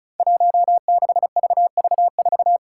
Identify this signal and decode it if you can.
Morse code, heard as a keyed tone, about 36 wpm, 700 Hz. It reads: 16VV4
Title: Five Challenging Letter-Number Combinations 35wpm